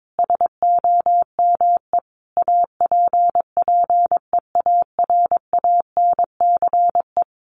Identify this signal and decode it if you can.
Morse code, heard as a keyed tone, about 22 wpm, 700 Hz. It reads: SOME APPEARANCE
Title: Sentences from Top 1100 Words 22wpm